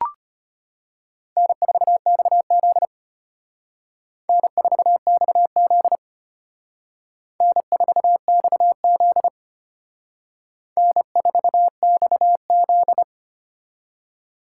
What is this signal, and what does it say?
Morse code, heard as a keyed tone, about 33 wpm, 700 Hz. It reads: N4XZ N4XZ N4XZ N4XZ